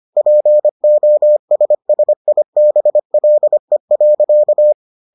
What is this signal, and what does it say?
Morse code, heard as a keyed tone, about 25 wpm, 600 Hz. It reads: POSSIBLE.